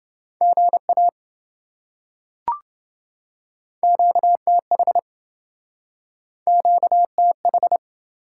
Morse code, heard as a keyed tone, about 27 wpm, 700 Hz. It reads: GA QTH QTH